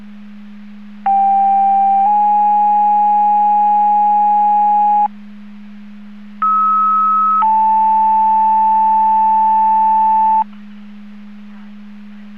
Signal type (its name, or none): two-tone page